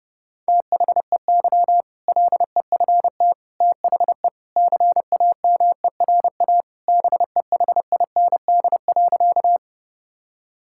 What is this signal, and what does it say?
Morse code, heard as a keyed tone, about 29 wpm, 700 Hz. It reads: THEY LEFT THE CAMERA BEHIND.
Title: Sentences from Top 1000 Words - Review Part 5 - 30wpm